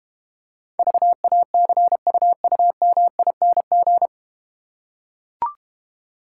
Morse code, heard as a keyed tone, about 32 wpm, 700 Hz. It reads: VACUUMING